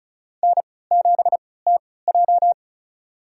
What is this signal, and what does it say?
Morse code, heard as a keyed tone, about 36 wpm, spaced as for 20 wpm, 700 Hz. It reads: N7TJ
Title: USA Call Signs - 4 Characters - 3x Character Spacing - 35wpm